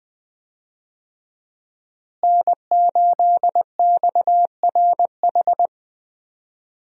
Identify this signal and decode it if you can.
Morse code, heard as a keyed tone, about 20 wpm, 700 Hz. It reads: N8XRH
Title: USA Call Signs - 5 Characters - Repeated 3x - 20wpm